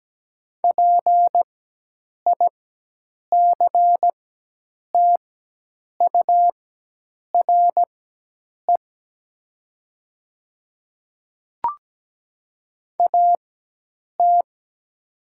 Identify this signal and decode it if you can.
Morse code, heard as a keyed tone, about 17 wpm, spaced as for 8 wpm, 700 Hz. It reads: PICTURE AT
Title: 4x Character Spacing - Top 200 Words - 17wpm